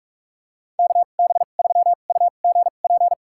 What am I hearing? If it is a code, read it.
Morse code, heard as a keyed tone, about 47 wpm, spaced as for 33 wpm, 700 Hz. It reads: KB3UGP
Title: USA Call Signs - 2x Character Spacing - 45wpm